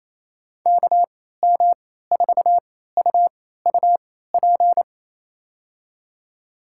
Morse code, heard as a keyed tone, about 28 wpm, spaced as for 16 wpm, 700 Hz. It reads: KM4UUP
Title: USA Call Signs - 3x Character Spacing - 28wpm